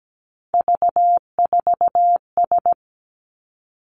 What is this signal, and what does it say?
Morse code, heard as a keyed tone, about 17 wpm, 700 Hz. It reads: V4S
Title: Three Challenging Letter-Number Combinations 17wpm